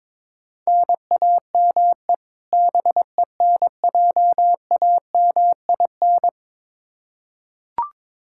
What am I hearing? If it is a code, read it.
Morse code, heard as a keyed tone, about 21 wpm, 700 Hz. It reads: NAME BENJAMIN
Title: K1USN SST QSO Practice 22wpm